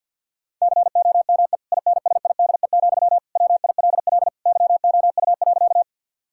Morse code, heard as a keyed tone, about 50 wpm, 700 Hz. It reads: COME INSIDE, WILL YOU.
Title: Sentences from Top 1000 Words - Review Part 6 - 50wpm